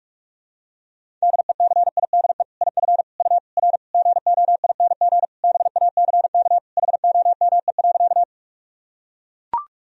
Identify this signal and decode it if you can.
Morse code, heard as a keyed tone, about 41 wpm, 700 Hz. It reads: DECIDE IF U R GOING BACK HOME.
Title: Sentences from Top 500 Words and Abbreviations 45wpm